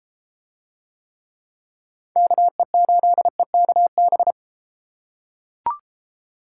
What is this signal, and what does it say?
Morse code, heard as a keyed tone, about 29 wpm, 700 Hz. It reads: KE8EKB